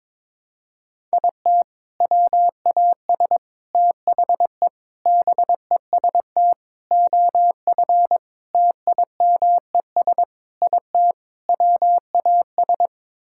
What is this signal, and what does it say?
Morse code, heard as a keyed tone, about 21 wpm, 700 Hz. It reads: IT WAS THE BEST OF TIMES IT WAS
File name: practice / 22wpm / Sets of 8 Words - 22wpm